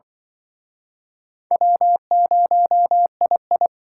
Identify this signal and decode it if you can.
Morse code, heard as a keyed tone, about 25 wpm, 700 Hz. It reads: W0II